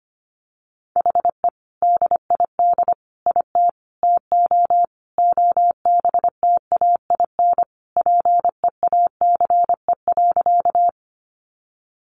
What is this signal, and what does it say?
Morse code, heard as a keyed tone, about 25 wpm, 700 Hz. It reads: HE DID IT TO OBTAIN PEACE.
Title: Sentences from Top 1500 Words 25wpm